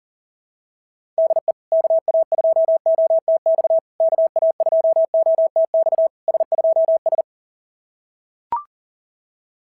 Morse code, heard as a keyed tone, about 40 wpm, 690 Hz. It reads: DE KA2OTX KA2OTX S2S
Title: SOTA QSO Practice 40wpm